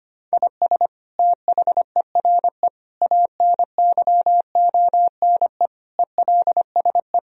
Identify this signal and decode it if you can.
Morse code, heard as a keyed tone, about 25 wpm, 700 Hz. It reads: IS THERE ANYONE ELSE
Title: Sentences from Top 1000 Words - Review Part 3 - 25wpm